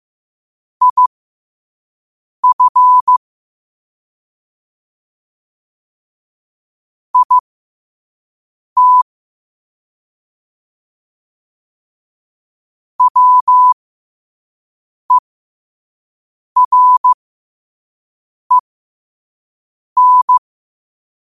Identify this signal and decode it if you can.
Morse code, heard as a keyed tone, about 15 wpm, spaced as for 5 wpm, 1000 Hz. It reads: IF IT WEREN
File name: qotdfiles / 5wpm